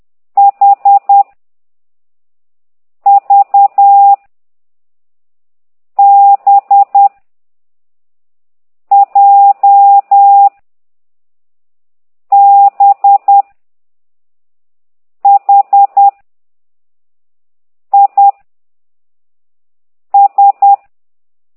Morse code, heard as a keyed tone, about 10 wpm, 800 Hz. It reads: H V B J B H I S
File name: ct2idw / exerc6